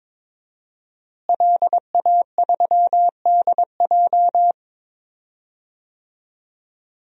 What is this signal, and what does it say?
Morse code, heard as a keyed tone, about 22 wpm, 700 Hz. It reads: LA3DJ
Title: DXCC-100 Call Signs - 5 Characters - Repeated 2x - 22wpm